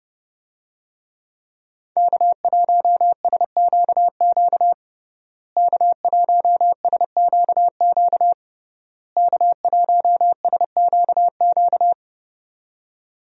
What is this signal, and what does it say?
Morse code, heard as a keyed tone, about 30 wpm, 700 Hz. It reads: K1SQQ K1SQQ K1SQQ
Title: USA Call Signs - 5 Characters - Repeated 3x - 30wpm